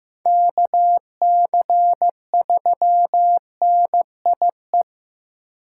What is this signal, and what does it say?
Morse code, heard as a keyed tone, about 15 wpm, 700 Hz. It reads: KC3NIE
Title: K1USN SST QSO Practice - 15wpm